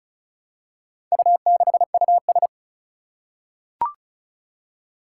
Morse code, heard as a keyed tone, about 36 wpm, 700 Hz. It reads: U6US
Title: Four Challenging Letter-Number Combinations 35wpm